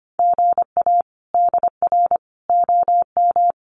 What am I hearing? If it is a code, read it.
Morse code, heard as a keyed tone, about 25 wpm, 700 Hz. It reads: GA DR OM
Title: Common QSO Phrases 25wpm